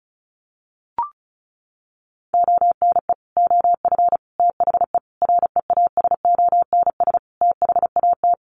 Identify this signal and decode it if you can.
Morse code, heard as a keyed tone, about 34 wpm, 700 Hz. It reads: ONE OF THE REASONS THAT
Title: Sets of 5 Words Encore 35wpm